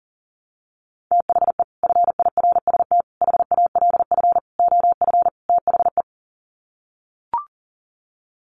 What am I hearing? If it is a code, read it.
Morse code, heard as a keyed tone, about 40 wpm, 700 Hz. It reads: THE FIRST HALF OF THE
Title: Sets of 5 Words 40wpm